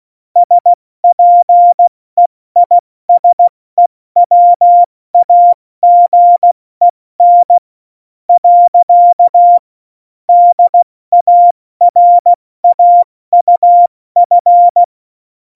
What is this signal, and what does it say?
Morse code, heard as a keyed tone, about 16 wpm, spaced as for 13 wpm, 700 Hz. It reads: SPEISEWAGEN. DARAUF
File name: Morsekurs / Witze / Witz 12. Februar 2019, 70 F80